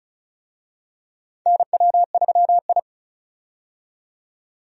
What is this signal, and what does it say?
Morse code, heard as a keyed tone, about 36 wpm, 700 Hz. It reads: NW3I
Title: USA Call Signs - 4 Characters - 35wpm